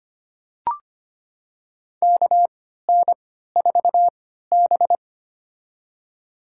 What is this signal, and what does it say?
Morse code, heard as a keyed tone, about 25 wpm, spaced as for 14 wpm, 700 Hz. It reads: KN4B
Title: USA Call Signs - 4 Characters - 3x Character Spacing - 25wpm